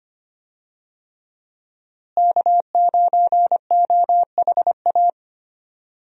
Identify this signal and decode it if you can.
Morse code, heard as a keyed tone, about 25 wpm, 700 Hz. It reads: K9OHA
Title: USA Call Signs - 5 Characters - Repeated 3x - 25wpm